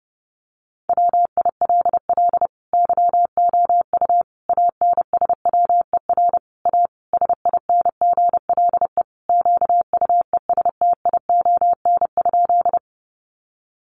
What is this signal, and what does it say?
Morse code, heard as a keyed tone, about 30 wpm, 700 Hz. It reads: WILL YOU ANSWER A SINGLE QUESTION?
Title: Sentences from Top 700 Words 30wpm